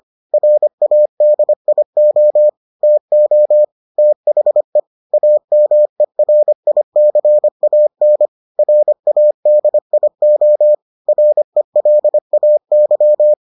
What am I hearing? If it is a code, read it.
Morse code, heard as a keyed tone, about 25 wpm, 600 Hz. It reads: RADIO TO THE AMERICAN RADIO RELAY